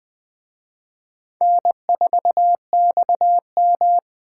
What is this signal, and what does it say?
Morse code, heard as a keyed tone, about 20 wpm, 700 Hz. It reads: N4XM